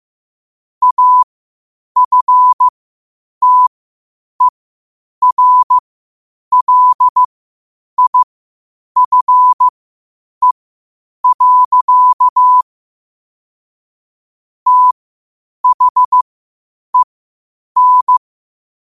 Morse code, heard as a keyed tone, about 15 wpm, spaced as for 8 wpm, 1000 Hz. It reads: AFTERLIFE. THEN
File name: qotdfiles / 8wpm